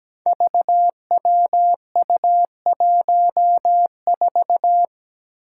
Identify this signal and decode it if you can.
Morse code, heard as a keyed tone, about 17 wpm, 700 Hz. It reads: VWU14
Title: Five Challenging Letter-Number Combinations 17wpm